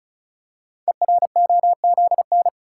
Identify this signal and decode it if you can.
Morse code, heard as a keyed tone, about 36 wpm, 700 Hz. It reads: EROZN
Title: Code Groups - 5 Letters - 35wpm